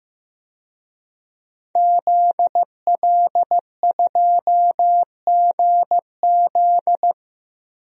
Morse code, heard as a keyed tone, about 15 wpm, 700 Hz. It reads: ZL2GZ
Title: DXCC-100 Call Signs - 5 Characters - Repeated 3x - 15wpm